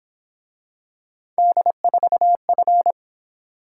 Morse code, heard as a keyed tone, about 26 wpm, 700 Hz. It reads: D4F
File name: Speed-Racing - Three Letter-Number Combinations - 22wpm